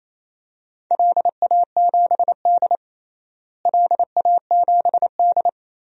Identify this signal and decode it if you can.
Morse code, heard as a keyed tone, about 28 wpm, 700 Hz. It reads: LA7D LA7D
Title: DXCC-100 Call Signs - 4 Characters - Repeated 2x - 28wpm